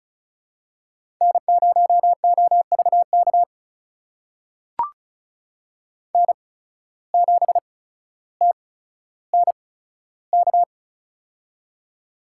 Morse code, heard as a keyed tone, about 36 wpm, spaced as for 15 wpm, 700 Hz. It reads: N0OVK N7TNK
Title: USA Call Signs - 5 Characters - 8x Character Spacing - 35wpm